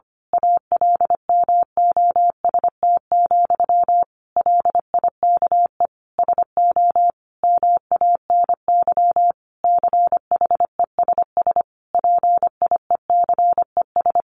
Morse code, heard as a keyed tone, about 25 wpm, 700 Hz. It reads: ALMOST, LIKE SO MANY CHESS PIECES
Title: A Sound Of Thunder 25wpm